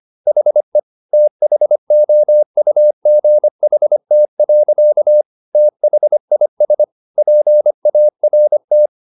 Morse code, heard as a keyed tone, about 25 wpm, 600 Hz. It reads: HE THOUGHT. THIS PART